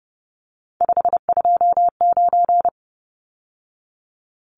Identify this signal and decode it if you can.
Morse code, heard as a keyed tone, about 30 wpm, 700 Hz. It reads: 529
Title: Numbers - 3-Digit 30wpm